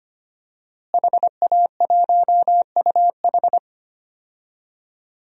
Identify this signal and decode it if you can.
Morse code, heard as a keyed tone, about 25 wpm, 700 Hz. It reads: HA1UH
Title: DXCC-100 Call Signs - 5 Characters - Repeated 3x - 25wpm